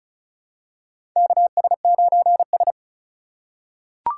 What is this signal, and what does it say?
Morse code, heard as a keyed tone, about 36 wpm, 690 Hz. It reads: KS9S
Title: USA Call Signs - 4 Characters - 2x Character Spacing - 35wpm